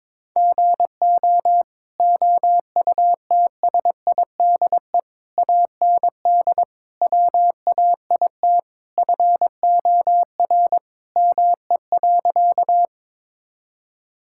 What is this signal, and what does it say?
Morse code, heard as a keyed tone, about 22 wpm, 700 Hz. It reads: GO OUTSIDE AND WAIT FOR ME.
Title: Sentences from Top 1000 Words - Review Part 6 - 22wpm